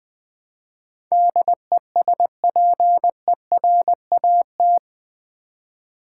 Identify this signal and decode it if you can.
Morse code, heard as a keyed tone, about 20 wpm, 700 Hz. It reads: DESPERAT
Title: Build A Word - 4 Letters At A Time - 11-Letter Words - 20wpm